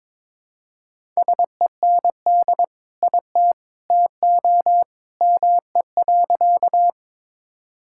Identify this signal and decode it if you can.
Morse code, heard as a keyed tone, about 22 wpm, 700 Hz. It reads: SEND IT TO ME.
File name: practice / 22wpm / Sentences with 2-Word Phrases - 22wpm